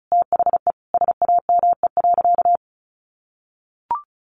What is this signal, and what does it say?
Morse code, heard as a keyed tone, about 34 wpm, 700 Hz. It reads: THE SAME.
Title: Sentences from Top 1300 Words 35wpm